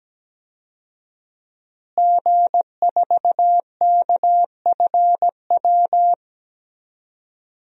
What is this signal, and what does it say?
Morse code, heard as a keyed tone, about 17 wpm, 700 Hz. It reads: G4KFW
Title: DXCC-100 Call Signs - 5 Characters - Repeated 2x - 17wpm